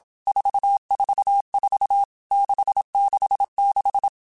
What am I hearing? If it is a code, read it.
Morse code, heard as a keyed tone, about 27 wpm, 800 Hz. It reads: 444 666
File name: Basic Letter Groups By7Groups / VB46 / VB46@25